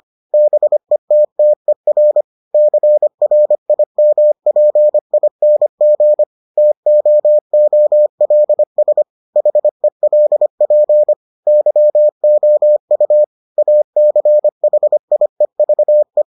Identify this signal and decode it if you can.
Morse code, heard as a keyed tone, about 25 wpm, 600 Hz. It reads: BETTER CRIMPING TOOLS HELP YOU ACHIEVE